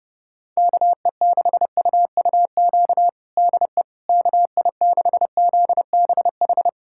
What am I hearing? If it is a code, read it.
Morse code, heard as a keyed tone, about 30 wpm, 700 Hz. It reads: KE6UUQ DE KI6ZBH
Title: USA Callsign DE Callsign 30wpm